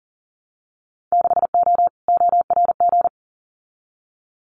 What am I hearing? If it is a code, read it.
Morse code, heard as a keyed tone, about 40 wpm, 700 Hz. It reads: 6O ORG
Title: Winter Field Day Exchanges Encore 40wpm